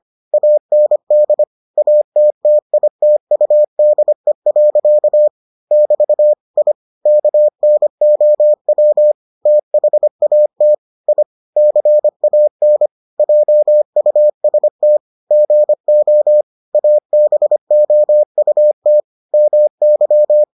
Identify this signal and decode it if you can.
Morse code, heard as a keyed tone, about 25 wpm, 600 Hz. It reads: AND ATTITUDE. <BT> I KNOW THAT I CAN JUST GO ABOUT MY